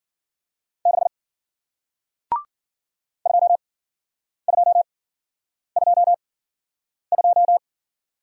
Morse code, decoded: 6 2222

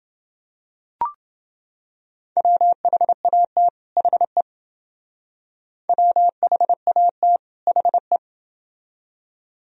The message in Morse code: WHAT HE WHAT HE

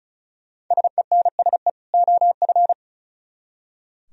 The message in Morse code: SENSE OF

